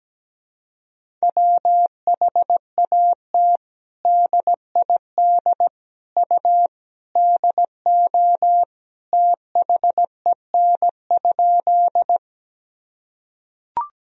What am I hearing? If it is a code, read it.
Morse code, heard as a keyed tone, about 17 wpm, 700 Hz. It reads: WHAT DID U DO THEN?
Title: Sentences from Top 100 Words and Abbreviations 17wpm